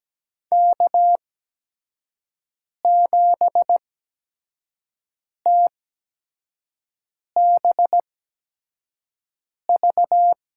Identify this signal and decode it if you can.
Morse code, heard as a keyed tone, about 17 wpm, spaced as for 5 wpm, 700 Hz. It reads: K7TBV
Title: USA Call Signs - 5 Characters - 8x Character Spacing - 17wpm